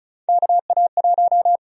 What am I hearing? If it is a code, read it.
Morse code, heard as a keyed tone, about 36 wpm, 700 Hz. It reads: KA1
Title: USA Call Sign Prefixes - Repeated 2x - 35wpm